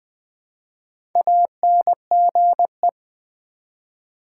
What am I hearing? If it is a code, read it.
Morse code, heard as a keyed tone, about 20 wpm, 700 Hz. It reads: ANGE
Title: Build A Word - 1 Letter At A Time - 5-Letter Words - 20wpm